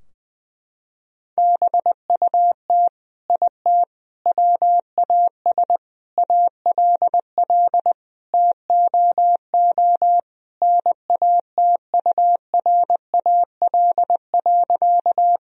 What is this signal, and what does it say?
Morse code, heard as a keyed tone, about 20 wpm, 700 Hz. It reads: BUT IT WAS ALL TOO NATURAL.
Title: Sentences from Top 1000 Words - Review Part 4 - 20wpm